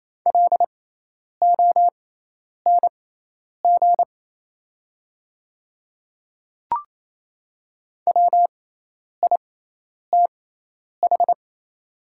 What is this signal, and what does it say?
Morse code, heard as a keyed tone, about 27 wpm, spaced as for 10 wpm, 700 Hz. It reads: LONG WITH
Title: 6x Character Spacing - Top 300 Words - 28wpm